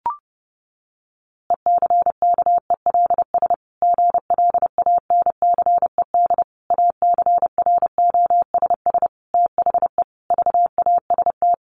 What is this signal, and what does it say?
Morse code, heard as a keyed tone, about 30 wpm, 700 Hz. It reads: ECKELS GLANCED ACROSS THE VAST
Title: A Sound Of Thunder 30wpm